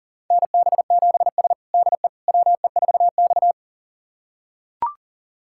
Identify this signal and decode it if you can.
Morse code, heard as a keyed tone, about 40 wpm, 700 Hz. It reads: NB7S DE WE4X
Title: USA Callsign DE Callsign - 4 Characters - 40wpm